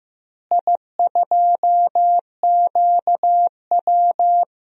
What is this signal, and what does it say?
Morse code, heard as a keyed tone, about 15 wpm, 700 Hz. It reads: I2QW
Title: DXCC-100 Call Signs - 4 Characters - Repeated 2x - 15wpm